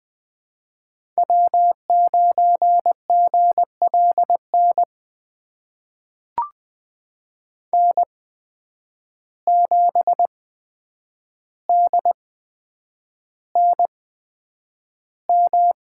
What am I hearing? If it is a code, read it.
Morse code, heard as a keyed tone, about 20 wpm, spaced as for 9 wpm, 700 Hz. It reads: W9GLN N7DNM
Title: USA Call Signs - 5 Characters - 8x Character Spacing - 20wpm